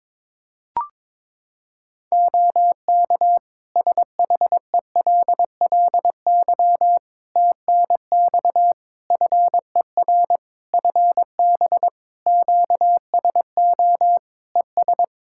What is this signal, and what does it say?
Morse code, heard as a keyed tone, about 22 wpm, 700 Hz. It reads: OK SHELLY TNX FER FB QSO ES
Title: Practice QSO - Exchange 4 - Ending - 22wpm